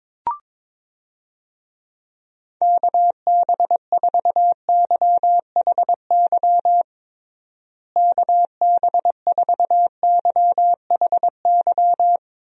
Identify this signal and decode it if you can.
Morse code, heard as a keyed tone, about 22 wpm, 700 Hz. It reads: KB4YHY KB4YHY
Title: USA Call Signs - 6 Characters - Repeated 3x - 22wpm